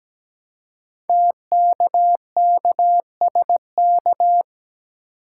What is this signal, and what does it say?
Morse code, heard as a keyed tone, about 17 wpm, 700 Hz. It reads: TKKSK
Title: Code Groups - 5 Letters - 17wpm